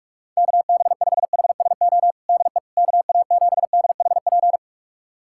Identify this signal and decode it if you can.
Morse code, heard as a keyed tone, about 41 wpm, 700 Hz. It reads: KB5HSO DE KA7DHP
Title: USA Callsign DE Callsign 45wpm